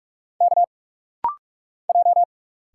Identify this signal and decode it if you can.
Morse code, heard as a keyed tone, about 45 wpm, 690 Hz. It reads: K J
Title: Single Letters - Rapid-Fire 45wpm